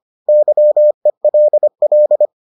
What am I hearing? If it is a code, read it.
Morse code, heard as a keyed tone, about 25 wpm, 600 Hz. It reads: YELL